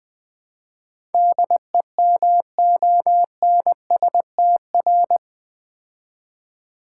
Morse code, heard as a keyed tone, about 20 wpm, 700 Hz. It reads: DEMONSTR
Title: Build A Word - 4 Letters At A Time - 11-Letter Words - 20wpm